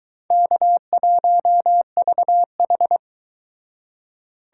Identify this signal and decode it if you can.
Morse code, heard as a keyed tone, about 23 wpm, 700 Hz. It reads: K1VH